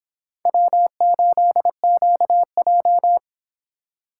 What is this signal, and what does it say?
Morse code, heard as a keyed tone, about 26 wpm, 700 Hz. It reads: W8QJ